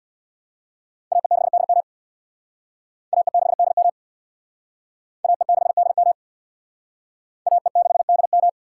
Morse code, heard as a keyed tone, about 61 wpm, 700 Hz. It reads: AE6DM AE6DM AE6DM AE6DM
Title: Speed Racing - USA 2x2 Call Signs 50wpm